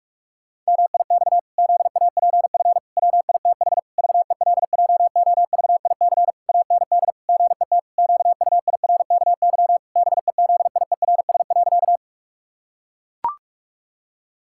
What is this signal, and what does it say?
Morse code, heard as a keyed tone, about 44 wpm, 700 Hz. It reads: MIX ZAPF WITH VELJOVIC AND GET QUIRKY BEZIERS.
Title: Pangram Challenge v2 - 45wpm